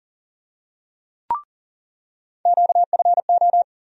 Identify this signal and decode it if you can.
Morse code, heard as a keyed tone, about 40 wpm, 700 Hz. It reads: QFO
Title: USA Call Sign Postfixes - 3 Letters 40wpm